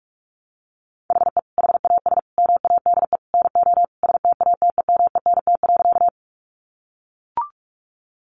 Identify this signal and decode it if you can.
Morse code, heard as a keyed tone, about 44 wpm, 700 Hz. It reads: HE HAS MADE NO STATEMENT.
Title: Sentences from Top 1200 Words 45wpm